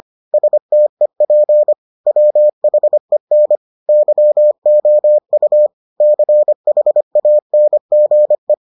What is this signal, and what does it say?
Morse code, heard as a keyed tone, about 25 wpm, 600 Hz. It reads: STEP WHEN YOU CHANGE